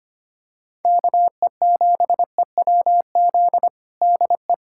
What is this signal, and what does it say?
Morse code, heard as a keyed tone, about 25 wpm, 700 Hz. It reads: KE7EWZ DE
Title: USA Callsign DE Callsign 25wpm